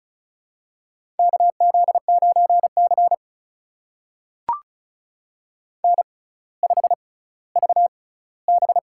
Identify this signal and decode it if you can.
Morse code, heard as a keyed tone, about 36 wpm, spaced as for 18 wpm, 700 Hz. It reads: KZ9C N5VB
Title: USA Call Signs - 4 Characters - 6x Character Spacing - 35wpm